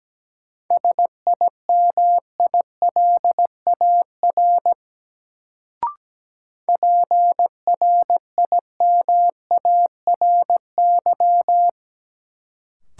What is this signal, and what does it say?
Morse code, heard as a keyed tone, about 17 wpm, 700 Hz. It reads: SIMILAR PRIMARY
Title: Sentences from Top 1000 Words - Review Part 5 - New Word Drill - 17wpm